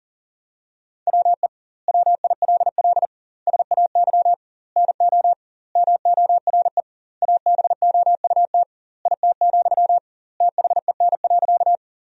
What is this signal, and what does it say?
Morse code, heard as a keyed tone, about 40 wpm, 700 Hz. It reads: WE WILL SAY NO MORE ABOUT IT, THEN.
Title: 2x Word-Spacing - Sentences from Top 200 Words - 40wpm